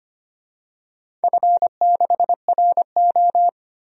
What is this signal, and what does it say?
Morse code, heard as a keyed tone, about 25 wpm, 700 Hz. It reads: F6RO
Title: DXCC-100 Call Signs - 4 Characters - 25wpm